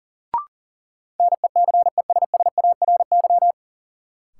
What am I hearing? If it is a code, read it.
Morse code, heard as a keyed tone, about 40 wpm, 700 Hz. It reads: NECESSARY
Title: Sentences from Top 800 Words - New Word Drill - 40wpm